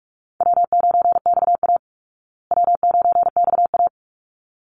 Morse code, heard as a keyed tone, about 44 wpm, 700 Hz. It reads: W9XA W9XA
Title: Call Signs Repeated 3x - 45wpm